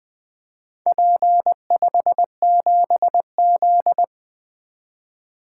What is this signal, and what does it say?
Morse code, heard as a keyed tone, about 20 wpm, 700 Hz. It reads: P57Z
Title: Four Letter-Number Combinations - 20wpm